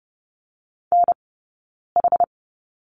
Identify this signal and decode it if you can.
Morse code, heard as a keyed tone, about 30 wpm, spaced as for 9 wpm, 690 Hz. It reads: NH
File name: Lesson 17d - Letter u - Two Characters 30wpm